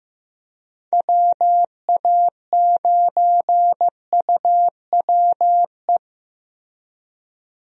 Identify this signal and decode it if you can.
Morse code, heard as a keyed tone, about 16 wpm, 700 Hz. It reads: WA9UWE
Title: USA Call Signs - 6 Characters - Repeated 3x - 15wpm